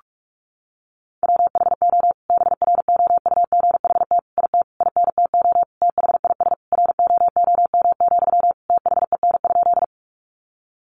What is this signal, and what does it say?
Morse code, heard as a keyed tone, about 44 wpm, 700 Hz. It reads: WHO BROUGHT IT INTO THIS ROOM, THEN?
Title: Sentences from Top 600 Words 45wpm